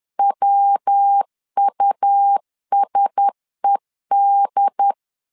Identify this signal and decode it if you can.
Morse code, heard as a keyed tone, about 10 wpm, 800 Hz. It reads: WUSED
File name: LEZIONI / 4 Q1S 10wpm 2133